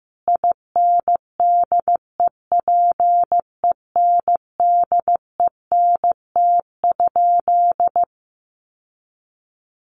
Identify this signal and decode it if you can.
Morse code, heard as a keyed tone, about 15 wpm, 700 Hz. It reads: INDEPENDENT?
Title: Sentences from Top 1400 Words 15wpm